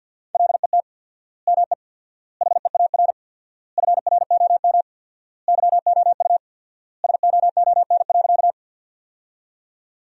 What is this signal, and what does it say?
Morse code, heard as a keyed tone, about 50 wpm, 700 Hz. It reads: LET ME HEAR FROM YOU SOON.
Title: 4x Word-Spacing - Sentences from Top 400 Words - 50wpm